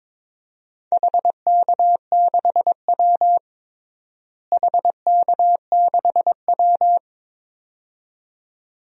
Morse code, heard as a keyed tone, about 22 wpm, 700 Hz. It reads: HK6W HK6W
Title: DXCC-100 Call Signs - 4 Characters - Repeated 2x - 22wpm